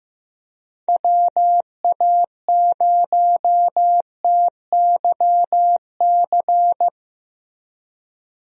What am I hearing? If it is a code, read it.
Morse code, heard as a keyed tone, about 16 wpm, 700 Hz. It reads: WA0TYC
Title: USA Call Signs - 6 Characters - Repeated 3x - 15wpm